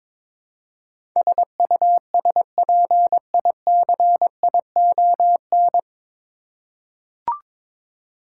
Morse code, heard as a keyed tone, about 22 wpm, 700 Hz. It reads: SUSPICION
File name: 9 Letter Words - First 3 Letters Given - 22wpm